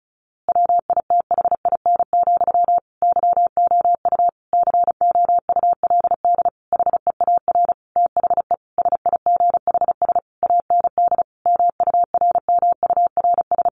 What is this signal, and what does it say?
Morse code, heard as a keyed tone, about 34 wpm, 700 Hz. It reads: WITHIN, YOU COULD HEAR THE SIGHS AND MURMURS
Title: A Sound Of Thunder 35wpm